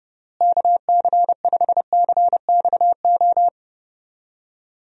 Morse code, heard as a keyed tone, about 30 wpm, 700 Hz. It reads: KC5CXO
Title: USA Call Signs - 6 Characters - Repeated 2x - 30wpm